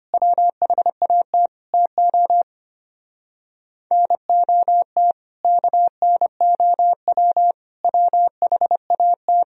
Morse code, heard as a keyed tone, about 27 wpm, 700 Hz. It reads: WHAT TO NOT KNOW WHAT
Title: Speed-Racing - Sets of 4 Words 25wpm